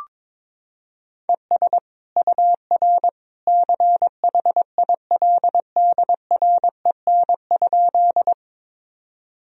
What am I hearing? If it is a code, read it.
Morse code, heard as a keyed tone, about 21 wpm, 700 Hz. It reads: ES UR CHILDREN?
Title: Sentences from Top 300 Words and Abbreviations 22wpm